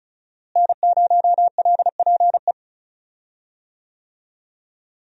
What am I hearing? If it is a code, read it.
Morse code, heard as a keyed tone, about 36 wpm, 700 Hz. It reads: N0LPE
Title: USA Call Signs - 5 Characters - Repeated 2x - 35wpm